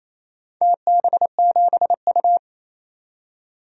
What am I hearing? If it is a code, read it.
Morse code, heard as a keyed tone, about 28 wpm, 700 Hz. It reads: TB7U
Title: DXCC-100 Call Signs - 4 Characters - 28wpm